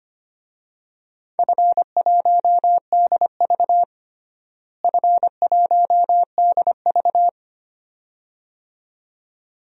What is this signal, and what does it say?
Morse code, heard as a keyed tone, about 25 wpm, 700 Hz. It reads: F1DV F1DV
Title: DXCC-100 Call Signs - 4 Characters - Repeated 2x - 25wpm